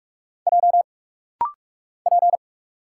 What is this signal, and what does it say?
Morse code, heard as a keyed tone, about 44 wpm, 700 Hz. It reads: J P
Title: Single Letter-Number Alt - Rapid-Fire 45wpm